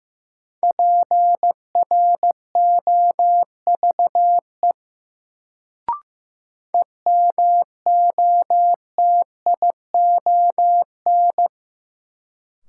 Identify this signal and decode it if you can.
Morse code, heard as a keyed tone, about 15 wpm, 700 Hz. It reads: PROVE EMOTION
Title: Sentences from Top 1600 Words - New Word Drill - 15wpm